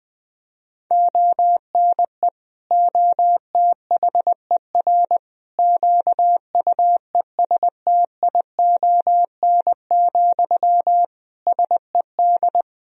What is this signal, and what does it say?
Morse code, heard as a keyed tone, about 20 wpm, 700 Hz. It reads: ONE OTHER QUESTION, SED
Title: Sentences from Top 400 Words and Abbreviations 20wpm